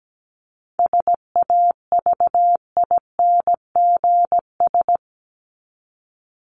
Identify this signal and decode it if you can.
Morse code, heard as a keyed tone, about 17 wpm, 700 Hz. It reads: SAVINGS
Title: Top 1001-1250 Words 17wpm